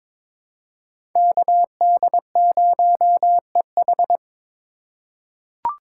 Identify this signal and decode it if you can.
Morse code, heard as a keyed tone, about 21 wpm, 700 Hz. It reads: KD0EH
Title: USA Call Signs - 3x Character Spacing - 22wpm